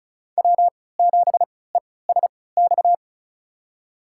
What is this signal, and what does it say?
Morse code, heard as a keyed tone, about 34 wpm, spaced as for 20 wpm, 700 Hz. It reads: W7ESX
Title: USA Call Signs - 5 Characters - 3x Character Spacing - 35wpm